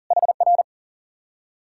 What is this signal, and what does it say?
Morse code, heard as a keyed tone, about 52 wpm, 700 Hz. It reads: HR